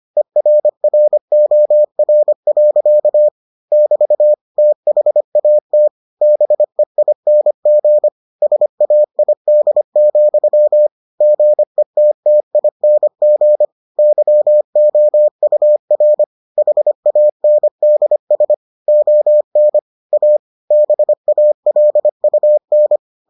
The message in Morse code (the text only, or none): ERROR. <BT> THAT BEING SAID, GETTING YOUR HANDS ON A BALUN